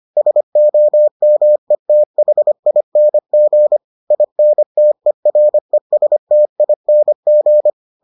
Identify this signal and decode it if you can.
Morse code, heard as a keyed tone, about 25 wpm, 600 Hz. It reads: SOMETHING INTERESTING